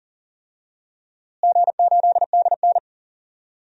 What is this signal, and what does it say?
Morse code, heard as a keyed tone, about 40 wpm, 700 Hz. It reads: G8DN
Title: DXCC-100 Call Signs - 4 Characters - Repeated 3x - 40wpm